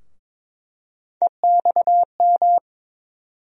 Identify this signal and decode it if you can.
Morse code, heard as a keyed tone, about 22 wpm, 700 Hz. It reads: EXM
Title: USA Call Sign Postfixes - 3 Letters 22wpm